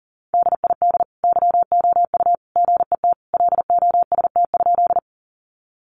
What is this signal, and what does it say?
Morse code, heard as a keyed tone, about 40 wpm, 700 Hz. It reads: DID YOU GET LOST?
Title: Sentences from Top 1400 Words 40wpm